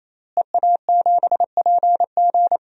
Morse code, heard as a keyed tone, about 28 wpm, 700 Hz. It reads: EA7PG
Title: DXCC-100 Call Signs - 5 Characters - Repeated 3x - 28wpm